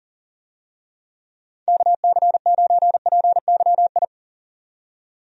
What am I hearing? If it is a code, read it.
Morse code, heard as a keyed tone, about 40 wpm, 700 Hz. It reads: KC9PYI
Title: USA Call Signs - 6 Characters - 40wpm